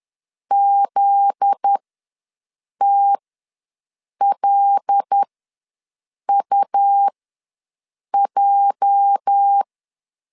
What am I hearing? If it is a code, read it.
Morse code, heard as a keyed tone, about 10 wpm, spaced as for 6 wpm, 800 Hz. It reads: ZTLUJ
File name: LEZIONI / 8 YJ45 / 8 YJ4 ws 10wpm 4020